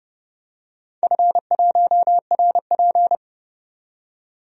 Morse code, heard as a keyed tone, about 30 wpm, 700 Hz. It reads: F1RP
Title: DXCC-100 Call Signs - 4 Characters - 30wpm